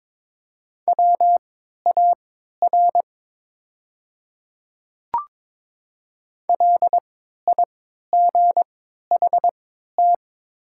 Morse code, heard as a keyed tone, about 22 wpm, spaced as for 12 wpm, 700 Hz. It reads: WAR LIGHT
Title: 3x Character Spacing - Top 500 Words - 22wpm